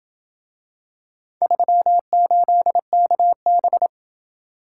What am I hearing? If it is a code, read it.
Morse code, heard as a keyed tone, about 28 wpm, 700 Hz. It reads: 38KB